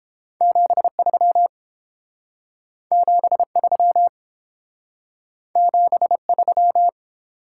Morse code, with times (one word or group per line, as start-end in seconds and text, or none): 0.4-1.5 73
2.9-4.1 73
5.6-6.9 73